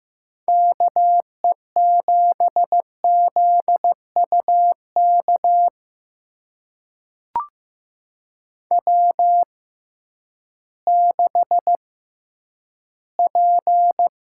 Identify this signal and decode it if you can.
Morse code, heard as a keyed tone, about 15 wpm, spaced as for 10 wpm, 700 Hz. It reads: KE7ZUK W6P
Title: USA Call Signs - 6x Character Spacing - 15wpm